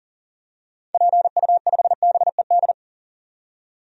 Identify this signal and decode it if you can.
Morse code, heard as a keyed tone, about 40 wpm, 700 Hz. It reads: PU5BED
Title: DXCC-100 Call Signs - 6 Characters - 40wpm